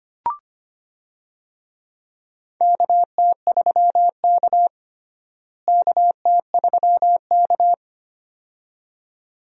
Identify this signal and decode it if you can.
Morse code, heard as a keyed tone, about 25 wpm, 700 Hz. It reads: KT3K KT3K
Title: USA Call Signs - 4 Characters - Repeated 2x - 25wpm